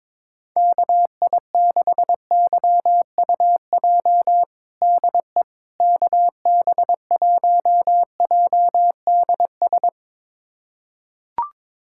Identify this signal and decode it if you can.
Morse code, heard as a keyed tone, about 22 wpm, 700 Hz. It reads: KI6YUJ DE KB1JDS
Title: USA Callsign DE Callsign 22wpm